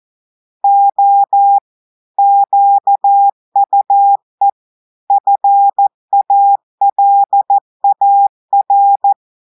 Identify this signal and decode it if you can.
Morse code, heard as a keyed tone, about 14 wpm, 800 Hz. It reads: O QUE FALAR
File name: audios / portugais / portugais texte en clair 2 vitesse 14 mots minutes